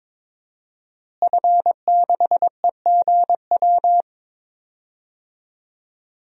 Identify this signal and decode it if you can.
Morse code, heard as a keyed tone, about 22 wpm, 700 Hz. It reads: F6EGW
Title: DXCC-100 Call Signs - 5 Characters - Repeated 3x - 22wpm